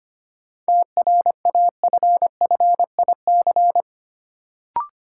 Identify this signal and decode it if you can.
Morse code, heard as a keyed tone, about 25 wpm, 700 Hz. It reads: TRAFFIC
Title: Sentences from Top 1300 Words - New Word Drill Alt - 25wpm